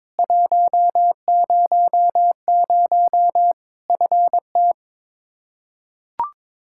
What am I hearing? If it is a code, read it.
Morse code, heard as a keyed tone, about 22 wpm, 700 Hz. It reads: 100 FT
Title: K1USN SST QSO Practice 22wpm